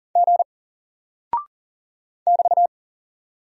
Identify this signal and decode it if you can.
Morse code, heard as a keyed tone, about 40 wpm, 700 Hz. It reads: G <BT>
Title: Single Character (Letter:Number:Punctuation:Prosign) Alt 40wpm